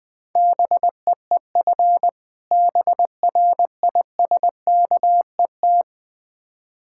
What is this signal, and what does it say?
Morse code, heard as a keyed tone, about 20 wpm, 700 Hz. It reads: BEEF BRISKET
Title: Difficult Words with Hints - 20wpm